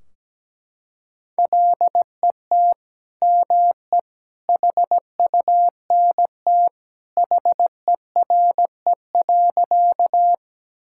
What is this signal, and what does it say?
Morse code, heard as a keyed tone, about 17 wpm, 700 Hz. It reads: LET ME HUNT HERE.
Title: Sentences from Top 1600 Words 17wpm